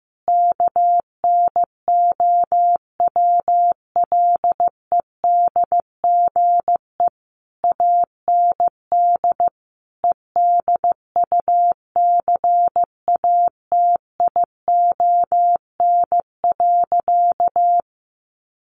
Morse code, heard as a keyed tone, about 15 wpm, 700 Hz. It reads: KNOWLEDGE AND EDUCATION.
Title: Sentences from Top 1500 Words 15wpm